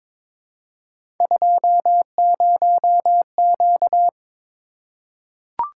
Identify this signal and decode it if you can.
Morse code, heard as a keyed tone, about 22 wpm, 700 Hz. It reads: 20Q
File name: practice / 22wpm / Three Letter-Number Combinations - 22wpm